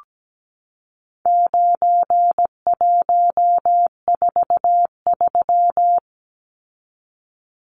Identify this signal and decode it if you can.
Morse code, heard as a keyed tone, about 17 wpm, 700 Hz. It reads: 9143
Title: Numbers - 4-Digit 17wpm